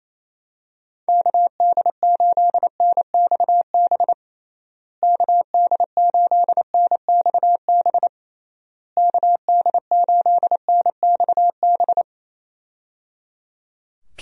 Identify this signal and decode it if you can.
Morse code, heard as a keyed tone, about 27 wpm, 700 Hz. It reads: KD8NXB KD8NXB KD8NXB
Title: USA Call Signs - 6 Characters - Repeated 3x - 28wpm